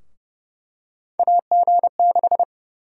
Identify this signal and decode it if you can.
Morse code, heard as a keyed tone, about 30 wpm, 700 Hz. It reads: AG6
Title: USA Call Sign Prefixes - 30wpm